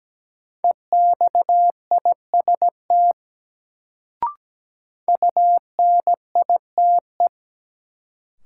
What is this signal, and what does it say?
Morse code, heard as a keyed tone, about 17 wpm, 700 Hz. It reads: EXIST UNITE
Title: Sentences from Top 1000 Words - Review Part 5 - New Word Drill - 17wpm